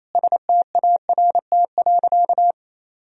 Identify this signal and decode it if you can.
Morse code, heard as a keyed tone, about 28 wpm, 700 Hz. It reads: START.
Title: Sentences from Top 1000 Words - Review Part 4 - 28wpm